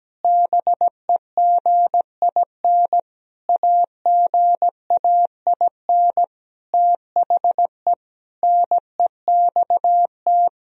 Morse code, heard as a keyed tone, about 17 wpm, 700 Hz. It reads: BEGIN AGAIN THE NEXT
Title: Sentences from Top 1000 Words - Review Part 1 - 17wpm